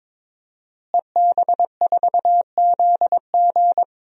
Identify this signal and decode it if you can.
Morse code, heard as a keyed tone, about 22 wpm, 700 Hz. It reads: EB4ZG
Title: DXCC-100 Call Signs - 5 Characters - Repeated 3x - 22wpm